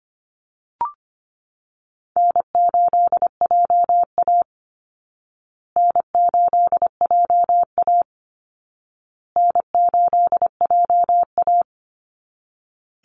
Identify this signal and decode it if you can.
Morse code, heard as a keyed tone, about 25 wpm, 700 Hz. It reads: N8JA N8JA N8JA
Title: Easy Call Signs Repeated 3x - 25wpm